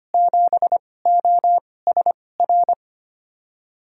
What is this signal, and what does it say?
Morse code, heard as a keyed tone, about 25 wpm, spaced as for 18 wpm, 690 Hz. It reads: 7OSR
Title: USA Call Signs - 6 Characters - 2x Character Spacing - 25wpm